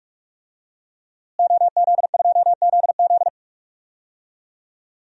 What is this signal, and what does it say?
Morse code, heard as a keyed tone, about 46 wpm, 700 Hz. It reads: OZ2ZZ
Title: DXCC-100 Call Signs - 5 Characters - Repeated 2x - 45wpm